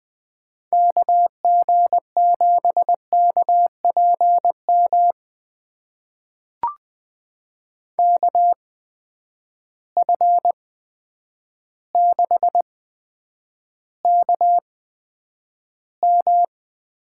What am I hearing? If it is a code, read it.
Morse code, heard as a keyed tone, about 20 wpm, spaced as for 9 wpm, 700 Hz. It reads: KG7KPM KF6KM
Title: USA Call Signs - 6 Characters - 8x Character Spacing - 20wpm